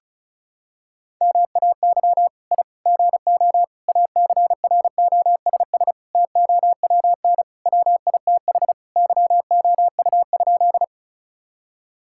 Morse code, heard as a keyed tone, about 36 wpm, 700 Hz. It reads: MAY I GO ACROSS TOWN WITH YOU?
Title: Sentences from Top 1000 Words - Review Part 5 - 35wpm